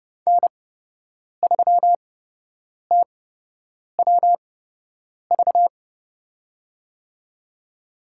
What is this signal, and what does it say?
Morse code, heard as a keyed tone, about 29 wpm, spaced as for 8 wpm, 700 Hz. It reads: N3TWV
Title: USA Call Signs - 5 Characters - 8x Character Spacing - 30wpm